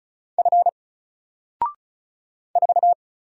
Morse code, heard as a keyed tone, about 36 wpm, 700 Hz. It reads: F 4
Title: Single Letter-Number Alt 35wpm